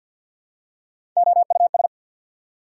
Morse code, heard as a keyed tone, about 54 wpm, 690 Hz. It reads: OUS
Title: Prefixes and Suffixes 50wpm